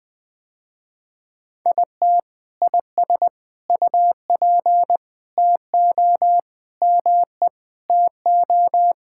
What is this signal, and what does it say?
Morse code, heard as a keyed tone, about 21 wpm, 700 Hz. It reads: IT IS UP TO ME TO
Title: Sentences with 2-Word Phrases - Spoken Beginning - 20wpm